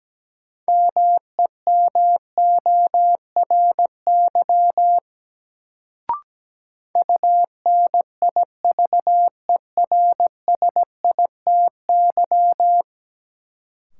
Morse code, heard as a keyed tone, about 17 wpm, 700 Hz. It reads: MEMORY UNIVERSITY
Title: Sentences from Top 1300 Words - New Word Drill Alt - 17wpm